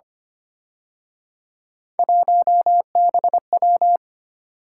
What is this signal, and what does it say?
Morse code, heard as a keyed tone, about 31 wpm, 700 Hz. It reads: 1BW
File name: Speed-Racing - Three Letter-Number Combinations - 25wpm